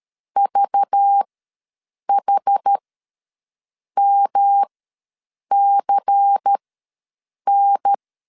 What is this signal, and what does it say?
Morse code, heard as a keyed tone, about 13 wpm, spaced as for 7 wpm, 800 Hz. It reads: VHMCN
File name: LEZIONI / 5 LBVC / 5 LBVC ws 12wpm 3159